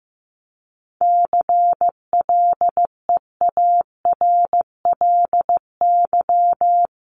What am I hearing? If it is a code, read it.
Morse code, heard as a keyed tone, about 15 wpm, 700 Hz. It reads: CLEARLY
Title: Top 1001-1250 Words 15wpm